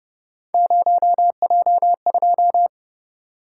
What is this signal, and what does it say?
Morse code, heard as a keyed tone, about 32 wpm, 700 Hz. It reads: 0J2